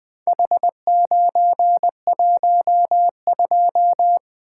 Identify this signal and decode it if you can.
Morse code, heard as a keyed tone, about 20 wpm, 700 Hz. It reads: H912
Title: Four Letter-Number Combinations - 20wpm